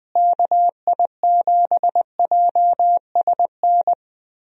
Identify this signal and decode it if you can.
Morse code, heard as a keyed tone, about 21 wpm, 700 Hz. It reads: KI7JSN
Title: USA Call Signs - 6 Characters - Repeated 3x - 20wpm